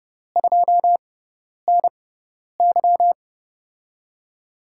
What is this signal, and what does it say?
Morse code, heard as a keyed tone, about 32 wpm, spaced as for 10 wpm, 700 Hz. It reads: 2NY
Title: USA Call Signs - 6x Character Spacing - 30wpm